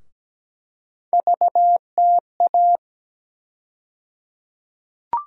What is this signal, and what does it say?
Morse code, heard as a keyed tone, about 17 wpm, 700 Hz. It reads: VTA
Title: USA Call Sign Postfixes - 3 Letters 17wpm